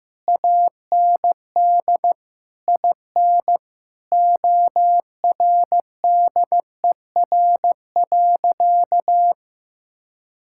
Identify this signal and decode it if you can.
Morse code, heard as a keyed tone, about 15 wpm, 700 Hz. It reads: AND IN ORDER.
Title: Sentences from Top 1000 Words - Review Part 6 - 15wpm